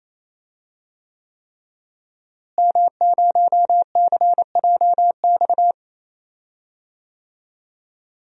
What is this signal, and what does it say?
Morse code, heard as a keyed tone, about 28 wpm, 690 Hz. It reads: M0CJX
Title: DXCC-100 Call Signs - 5 Characters - Repeated 3x - 28wpm